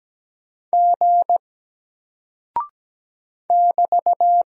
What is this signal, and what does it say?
Morse code, heard as a keyed tone, about 17 wpm, 700 Hz. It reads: G <BT>
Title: Single Character (Letter:Number:Punctuation:Prosign) Alt 17wpm